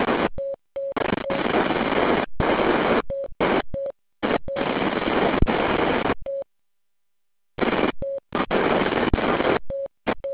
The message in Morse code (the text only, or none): none